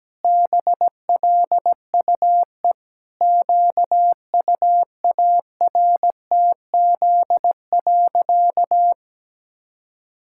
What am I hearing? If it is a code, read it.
Morse code, heard as a keyed tone, about 17 wpm, 700 Hz. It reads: BLUE QUARTZ.
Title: Pangram Challenge v2 - 17wpm